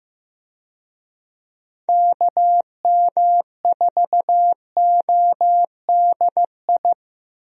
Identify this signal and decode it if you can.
Morse code, heard as a keyed tone, about 15 wpm, 700 Hz. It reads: KM4ODI
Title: USA Call Signs - 6 Characters - Repeated 2x - 15wpm